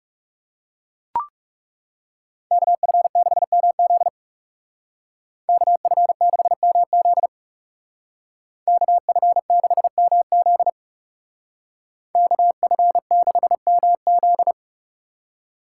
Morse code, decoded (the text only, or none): KF6MZ KF6MZ KF6MZ KF6MZ